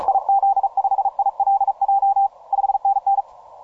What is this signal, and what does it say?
Morse code, heard as a keyed tone, about 37 wpm, 790 Hz. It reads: IZ5ILJ 5NN
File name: simple-file-list / 5A5A 40MCW